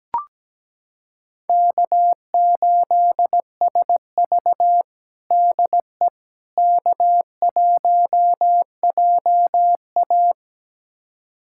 Morse code, heard as a keyed tone, about 17 wpm, 700 Hz. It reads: K8SV DE K1JA
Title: USA Callsign DE Callsign - 4 Characters - 17wpm